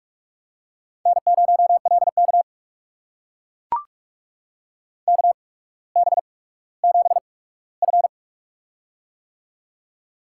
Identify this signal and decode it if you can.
Morse code, heard as a keyed tone, about 44 wpm, spaced as for 19 wpm, 700 Hz. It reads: N0LK KB7F
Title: USA Call Signs - 4 Characters - 8x Character Spacing - 45wpm